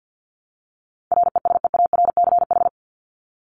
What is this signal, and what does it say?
Morse code, heard as a keyed tone, about 50 wpm, 700 Hz. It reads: RESEARCH
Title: Top 501-750 Words 50wpm